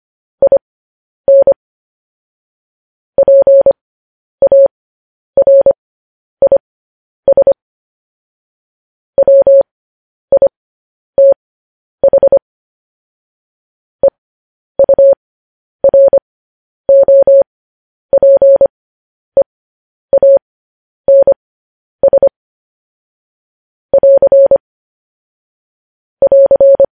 Morse code, heard as a keyed tone, about 25 wpm, spaced as for 10 wpm, 550 Hz. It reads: IN PARIS WITH EUROPEANS <AR> <AR>
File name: podcast / News headlines-1x-25x10